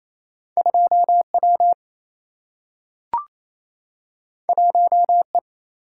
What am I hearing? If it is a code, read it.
Morse code, heard as a keyed tone, about 28 wpm, 700 Hz. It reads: 2 W 1 E
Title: Single-Letter Single-Number 28wpm